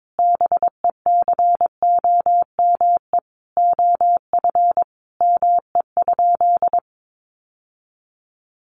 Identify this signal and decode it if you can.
Morse code, heard as a keyed tone, about 22 wpm, 700 Hz. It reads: BECOME OF ME?
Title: Sentences from Top 600 Words 22wpm